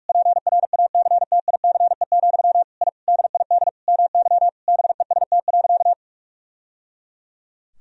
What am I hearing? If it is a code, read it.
Morse code, heard as a keyed tone, about 41 wpm, 700 Hz. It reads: PRACTICE, I DID MY BEST.
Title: Sentences from Top 1000 Words - Review Part 4 - 45wpm